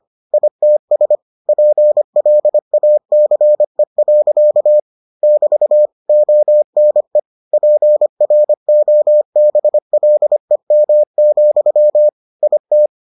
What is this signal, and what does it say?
Morse code, heard as a keyed tone, about 25 wpm, 600 Hz. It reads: ITS PLACE. <BT> ONE PROBLEM, IT